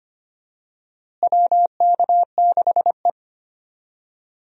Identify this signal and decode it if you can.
Morse code, heard as a keyed tone, about 25 wpm, 700 Hz. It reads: WK6E
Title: USA Call Signs - 4 Characters - Repeated 3x - 25wpm